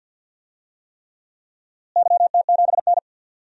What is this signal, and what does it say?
Morse code, heard as a keyed tone, about 50 wpm, 700 Hz. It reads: YT7N
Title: DXCC-100 Call Signs - 4 Characters - Repeated 3x - 50wpm